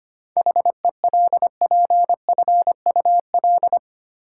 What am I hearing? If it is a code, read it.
Morse code, heard as a keyed tone, about 25 wpm, 700 Hz. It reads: HELPFUL
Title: Seven Letter Words - 25wpm